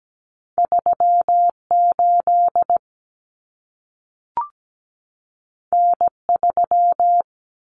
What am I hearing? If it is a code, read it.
Morse code, heard as a keyed tone, about 17 wpm, 700 Hz. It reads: 38 N3
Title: Common Cut Numbers - 2-Digit 17wpm